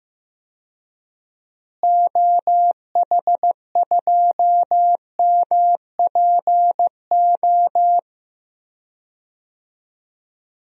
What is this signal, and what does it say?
Morse code, heard as a keyed tone, about 16 wpm, 700 Hz. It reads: OH2MPO
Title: DXCC-100 Call Signs - 6 Characters - Repeated 3x - 15wpm